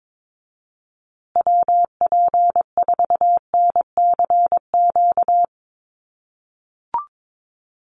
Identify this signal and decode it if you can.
Morse code, heard as a keyed tone, about 22 wpm, 700 Hz. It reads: WP4NCQ
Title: USA Call Signs Repeated 2x 22wpm